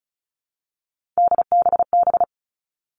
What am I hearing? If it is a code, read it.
Morse code, heard as a keyed tone, about 34 wpm, 700 Hz. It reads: DBB
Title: All Cut Numbers - 3-Digit 35wpm